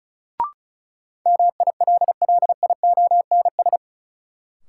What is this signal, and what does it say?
Morse code, heard as a keyed tone, about 36 wpm, 700 Hz. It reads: MILLIONS
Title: Sets of 5 Words - New Word Drill - 35wpm